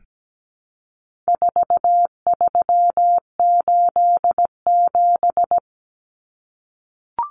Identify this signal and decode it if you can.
Morse code, heard as a keyed tone, about 17 wpm, 700 Hz. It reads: 4387
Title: Numbers - 4-Digit 17wpm